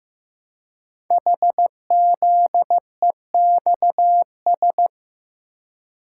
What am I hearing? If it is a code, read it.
Morse code, heard as a keyed tone, about 15 wpm, 700 Hz. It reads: HZEXS
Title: Code Groups - 5 Letters - 15wpm